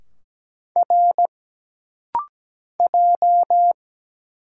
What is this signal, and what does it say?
Morse code, heard as a keyed tone, about 17 wpm, 700 Hz. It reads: R J
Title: Single Letters - Rapid-Fire 17wpm